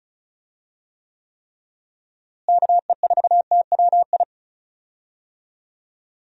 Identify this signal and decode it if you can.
Morse code, heard as a keyed tone, about 34 wpm, 700 Hz. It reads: KE4TWI
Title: USA Call Signs - 6 Characters - Repeated 3x - 35wpm